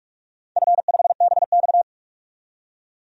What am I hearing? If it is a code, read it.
Morse code, heard as a keyed tone, about 46 wpm, 700 Hz. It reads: F5BX
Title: DXCC-100 Call Signs - 4 Characters - 45wpm